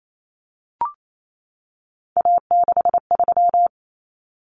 Morse code, heard as a keyed tone, about 28 wpm, 700 Hz. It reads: A63
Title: Common Cut Numbers - 3-Digit 28wpm